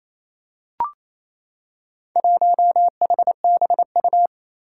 Morse code, heard as a keyed tone, about 27 wpm, 700 Hz. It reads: 1HBU
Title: Four Challenging Letter-Number Combinations 28wpm